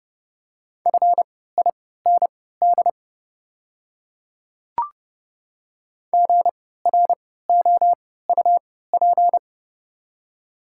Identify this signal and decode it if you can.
Morse code, heard as a keyed tone, about 32 wpm, spaced as for 17 wpm, 700 Hz. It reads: FIND GROUP
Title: 3x Character Spacing - Top 300 Words - 30wpm